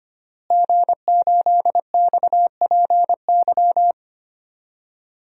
Morse code, heard as a keyed tone, about 25 wpm, 700 Hz. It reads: G8XPY
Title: DXCC-100 Call Signs - 5 Characters - Repeated 3x - 25wpm